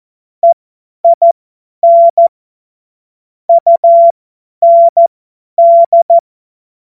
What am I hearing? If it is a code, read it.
Morse code, heard as a keyed tone, about 14 wpm, spaced as for 10 wpm, 680 Hz. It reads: EIN UND
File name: Morsekurs / Witze / Witz 11. Mai 2021, 50 F70